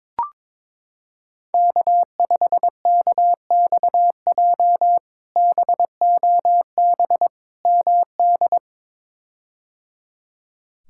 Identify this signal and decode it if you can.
Morse code, heard as a keyed tone, about 22 wpm, 700 Hz. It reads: K5KXJ BOB MD
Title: K1USN SST QSO Practice - 22wpm